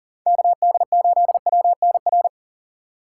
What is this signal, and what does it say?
Morse code, heard as a keyed tone, about 40 wpm, 700 Hz. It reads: KD8WNR
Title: USA Call Signs - 6 Characters - 2x Character Spacing - 40wpm